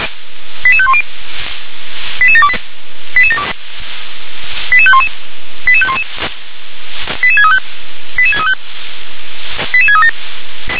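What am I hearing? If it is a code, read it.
Five-tone selective call (ZVEI), heard as quick runs of five tones, tones of about 70 ms. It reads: 88510 88728 88511 88511 88635 88635 88637
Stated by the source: ZVEI five-tone selective call received on an SDR.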